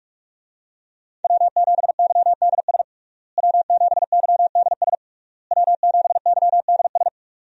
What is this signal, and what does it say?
Morse code, heard as a keyed tone, about 46 wpm, 700 Hz. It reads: W7YDS W7YDS W7YDS
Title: USA Call Signs - 5 Characters - Repeated 3x - 45wpm